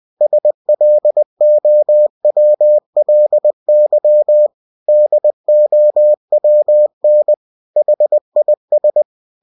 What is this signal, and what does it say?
Morse code, heard as a keyed tone, about 20 wpm, 600 Hz. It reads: SLOWLY DOWN HIS
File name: cw-books / winnie-the-pooh / winnie-20wpm-page-0025